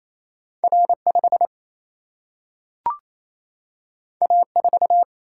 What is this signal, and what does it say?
Morse code, heard as a keyed tone, about 28 wpm, 700 Hz. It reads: R5 A4
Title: Single-Letter Single-Number 28wpm